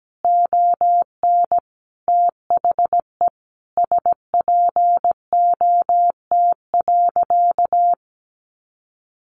Morse code, heard as a keyed tone, about 17 wpm, 700 Hz. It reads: ON THE SPOT.
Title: Sentences from Top 1100 Words 17wpm